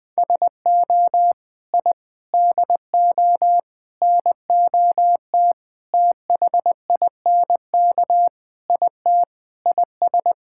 Morse code, heard as a keyed tone, about 20 wpm, 700 Hz. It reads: SO I DO NOT THINK IT IS
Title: Sentences with 5-Word Phrases - 20wpm